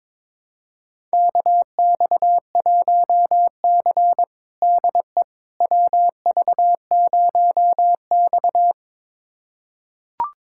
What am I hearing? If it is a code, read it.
Morse code, heard as a keyed tone, about 22 wpm, 700 Hz. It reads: KX1C DE WV0X
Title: USA Callsign DE Callsign - 4 Characters - 22wpm